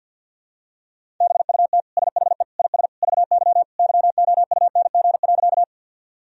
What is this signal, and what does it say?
Morse code, heard as a keyed tone, about 45 wpm, 700 Hz. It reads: BUT SHE IS VY YOUNG.
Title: Sentences from Top 400 Words and Abbreviations 50wpm